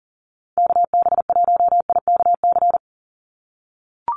Morse code, heard as a keyed tone, about 40 wpm, 700 Hz. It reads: KB1IKC
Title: USA Call Signs 40wpm